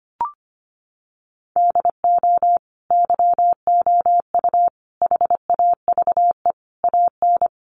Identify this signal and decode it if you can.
Morse code, heard as a keyed tone, about 25 wpm, 700 Hz. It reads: DO YOU HAVE AN
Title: Sentences from Top 1500 Words 25wpm